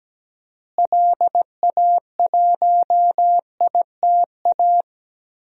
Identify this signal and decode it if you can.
Morse code, heard as a keyed tone, about 17 wpm, 700 Hz. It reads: LA1ITA
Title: DXCC-100 Call Signs - 6 Characters - Repeated 3x - 17wpm